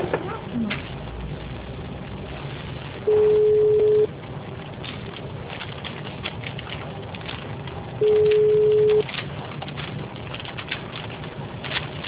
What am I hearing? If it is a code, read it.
Ringback tone (Europe, 425 Hz), with speech.